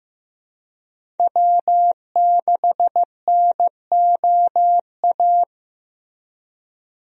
Morse code, heard as a keyed tone, about 16 wpm, 700 Hz. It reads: W6NOA
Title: USA Call Signs - 5 Characters - Repeated 2x - 15wpm